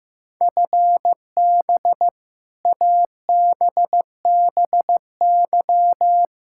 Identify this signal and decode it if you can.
Morse code, heard as a keyed tone, about 15 wpm, 700 Hz. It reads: FB ABBY
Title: Practice QSO - Exchange 3 - More Info - 15wpm